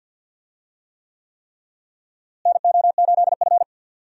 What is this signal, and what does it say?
Morse code, heard as a keyed tone, about 50 wpm, 700 Hz. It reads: NO8F
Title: USA Call Signs - 4 Characters - Repeated 3x - 50wpm